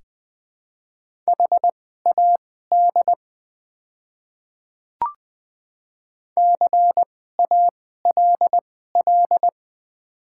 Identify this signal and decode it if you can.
Morse code, heard as a keyed tone, about 20 wpm, spaced as for 14 wpm, 700 Hz. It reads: HAD CALL
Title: 2x Character Spacing - Top 100 Words - 20wpm